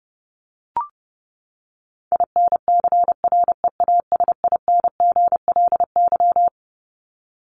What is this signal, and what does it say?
Morse code, heard as a keyed tone, about 30 wpm, 700 Hz. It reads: INCREASINGLY
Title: LY Suffix - Top 100 Words with Base Words 30wpm